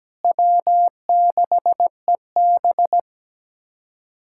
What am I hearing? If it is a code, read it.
Morse code, heard as a keyed tone, about 17 wpm, 700 Hz. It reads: W6EB
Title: USA Callsign DE Callsign - 4 Characters - 17wpm